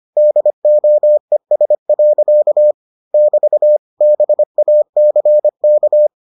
Morse code, heard as a keyed tone, about 25 wpm, 600 Hz. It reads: DOES. <BT> BACK